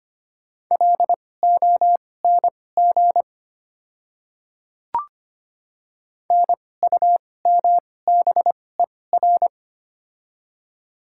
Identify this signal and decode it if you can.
Morse code, heard as a keyed tone, about 25 wpm, spaced as for 18 wpm, 700 Hz. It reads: LONG NUMBER
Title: 2x Character Spacing - Top 100 Words - 25wpm